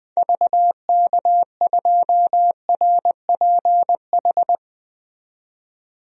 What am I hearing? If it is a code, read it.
Morse code, heard as a keyed tone, about 20 wpm, 700 Hz. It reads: VK2RPH
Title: DXCC-100 Call Signs - 6 Characters - 20wpm